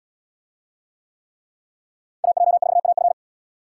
Morse code, heard as a keyed tone, about 67 wpm, 700 Hz. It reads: AJ4AF